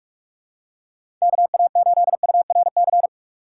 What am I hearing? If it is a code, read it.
Morse code, heard as a keyed tone, about 44 wpm, 700 Hz. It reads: KA8URC
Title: USA Call Signs - 6 Characters - 6x Character Spacing - 45wpm